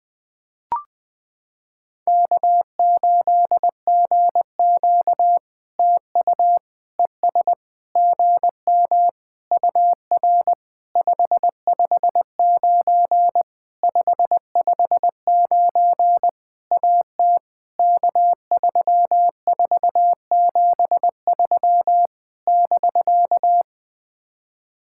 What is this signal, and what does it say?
Morse code, heard as a keyed tone, about 20 wpm, 700 Hz. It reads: K8GQ TU ES GM UR 559 559 AT K3473 <BK>
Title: POTA QSO Practice - 20wpm